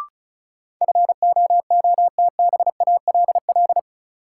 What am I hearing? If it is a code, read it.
Morse code, heard as a keyed tone, about 34 wpm, 700 Hz. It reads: FOOTBALL
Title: Sentences from Top 1600 Words - New Word Drill - 35wpm